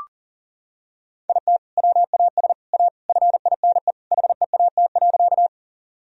Morse code, heard as a keyed tone, about 40 wpm, 700 Hz. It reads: IT WAS A FINE HEAT.
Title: Sentences from Top 1000 Words - Review Part 3 - 40wpm